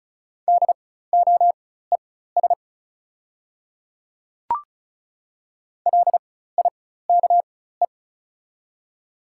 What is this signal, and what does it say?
Morse code, heard as a keyed tone, about 36 wpm, spaced as for 16 wpm, 690 Hz. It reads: DOES LIKE
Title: 4x Character Spacing - Top 500 Words - 35wpm